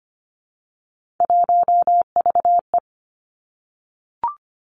Morse code, heard as a keyed tone, about 25 wpm, 700 Hz. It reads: 1VE
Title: Three Challenging Letter-Number Combinations 25wpm